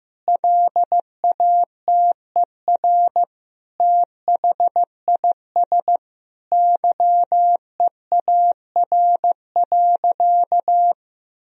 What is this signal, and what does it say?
Morse code, heard as a keyed tone, about 15 wpm, 700 Hz. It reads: LATER THIS YEAR.
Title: Sentences from Top 1000 Words - Review Part 3 - 15wpm